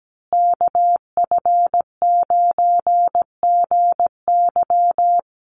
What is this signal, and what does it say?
Morse code, heard as a keyed tone, about 17 wpm, 700 Hz. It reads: KF9GY
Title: USA Call Signs Repeated 2x 17wpm